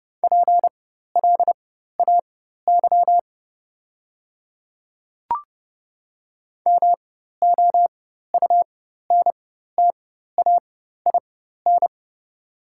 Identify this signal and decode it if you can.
Morse code, heard as a keyed tone, about 30 wpm, spaced as for 14 wpm, 700 Hz. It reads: PLAY MOUNTAIN
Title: 4x Character Spacing - Top 300 Words - 30wpm